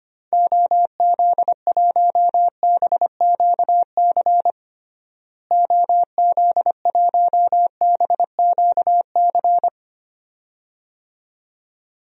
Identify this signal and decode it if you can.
Morse code, heard as a keyed tone, about 25 wpm, 700 Hz. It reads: OZ1BQC OZ1BQC
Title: DXCC-100 Call Signs - 6 Characters - Repeated 2x - 25wpm